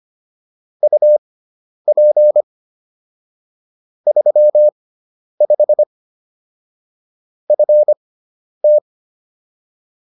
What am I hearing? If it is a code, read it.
Morse code, heard as a keyed tone, about 25 wpm, spaced as for 10 wpm, 600 Hz. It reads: UP 35 FT